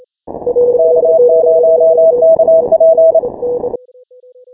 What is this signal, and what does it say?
Morse code, heard as a keyed tone, about 28 wpm, 650 Hz, with noise. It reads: K0KP
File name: final / W1UJ-168-140-MorseRunner